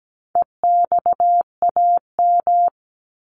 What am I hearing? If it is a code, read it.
Morse code, heard as a keyed tone, about 17 wpm, 700 Hz. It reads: EXAM
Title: Top 501-750 Words 17wpm